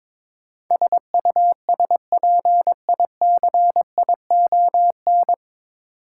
Morse code, heard as a keyed tone, about 21 wpm, 700 Hz. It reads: SUSPICION
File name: 9 Letter Words - First 3 Letters Given - 22wpm